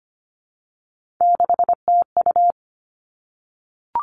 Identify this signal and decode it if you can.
Morse code, heard as a keyed tone, about 25 wpm, 700 Hz. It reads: 6TU
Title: Three Challenging Letter-Number Combinations 25wpm